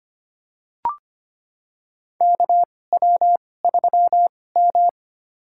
Morse code, heard as a keyed tone, about 25 wpm, spaced as for 18 wpm, 700 Hz. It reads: KW3M
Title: USA Call Signs - 2x Character Spacing - 25wpm